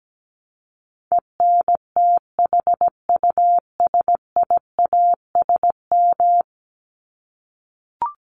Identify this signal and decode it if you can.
Morse code, heard as a keyed tone, about 17 wpm, 700 Hz. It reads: ENTHUSIASM
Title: Top 1751-2000 Words 17wpm